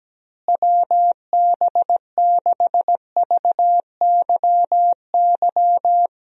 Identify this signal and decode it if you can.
Morse code, heard as a keyed tone, about 17 wpm, 700 Hz. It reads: WB6VYY
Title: USA Call Signs - 6 Characters - Repeated 2x - 17wpm